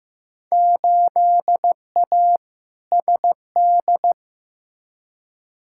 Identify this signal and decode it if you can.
Morse code, heard as a keyed tone, about 15 wpm, 700 Hz. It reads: 8A SD
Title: Field Day Exchanges Encore - 15wpm